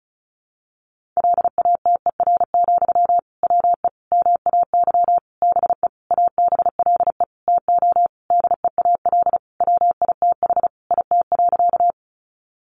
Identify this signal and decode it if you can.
Morse code, heard as a keyed tone, about 34 wpm, 700 Hz. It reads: LATER, WE MAY BE ABLE TO DEAL WITH IT.
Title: Sentences from Top 800 Words 35wpm